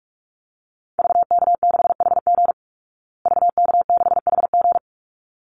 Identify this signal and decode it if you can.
Morse code, heard as a keyed tone, about 44 wpm, 700 Hz. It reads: VK6HG VK6HG
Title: Call Signs Repeated 3x - 45wpm